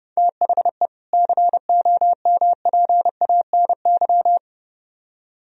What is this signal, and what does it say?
Morse code, heard as a keyed tone, about 39 wpm, 700 Hz. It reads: THE COMPANY